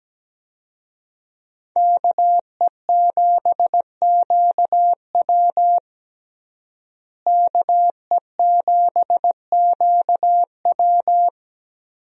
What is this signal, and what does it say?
Morse code, heard as a keyed tone, about 17 wpm, 700 Hz. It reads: KE7QW KE7QW
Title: USA Call Signs - 5 Characters - Repeated 2x - 17wpm